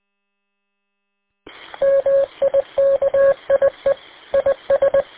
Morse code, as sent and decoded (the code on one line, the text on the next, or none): -- .. -.- .. / .. ...
MIKIE IS